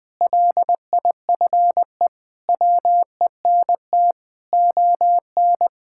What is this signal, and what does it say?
Morse code, heard as a keyed tone, about 20 wpm, 700 Hz. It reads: LIFE WENT ON
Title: Sentences with 5-Word Phrases - Spoken Beginning - 20wpm